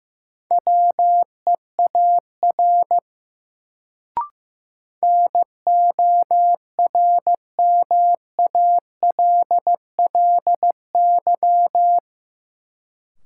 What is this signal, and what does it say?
Morse code, heard as a keyed tone, about 15 wpm, 700 Hz. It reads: WEAR NORMALLY
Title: Sentences from Top 1000 Words - Review Part 6 - New Word Drill - 15wpm